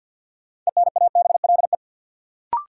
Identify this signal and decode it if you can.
Morse code, heard as a keyed tone, about 53 wpm, 700 Hz. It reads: ENABLE
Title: Sentences from Top 1500 Words - New Word Drill - 50wpm